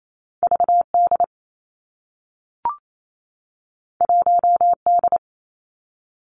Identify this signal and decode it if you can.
Morse code, heard as a keyed tone, about 28 wpm, 700 Hz. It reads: V D 1 D
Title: Two Challenging Letter-Number Combinations 28wpm